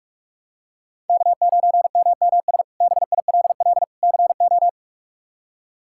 Morse code, heard as a keyed tone, about 46 wpm, 700 Hz. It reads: K9MMS BILL CO
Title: K1USN SST QSO Practice - 45wpm